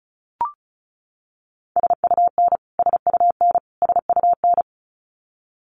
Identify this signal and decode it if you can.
Morse code, heard as a keyed tone, about 34 wpm, 700 Hz. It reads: SUN SUN SUN
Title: Top 500 Words Repeated 35wpm